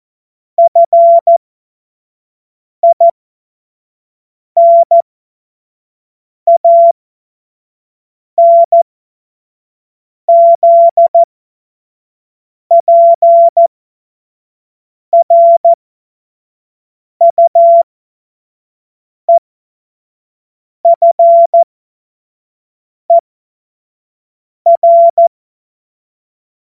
Morse code, heard as a keyed tone, about 14 wpm, spaced as for 5 wpm, 680 Hz. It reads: FINANZPRUEFER
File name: Morsekurs / Witze / Witz 18. Juli 2023 25 F70